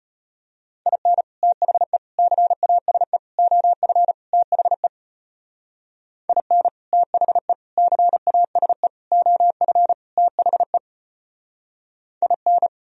Morse code, heard as a keyed tone, about 33 wpm, 700 Hz. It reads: IN THE CASE OF THE IN THE CASE OF THE IN